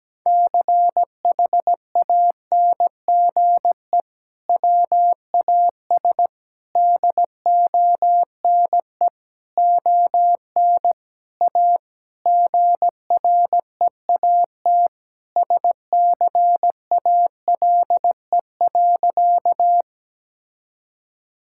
Morse code, heard as a keyed tone, about 17 wpm, 700 Hz. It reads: CHANGE WAS DONE ON A GREAT SCALE.
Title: Sentences from Top 1000 Words - Review Part 5 - 17wpm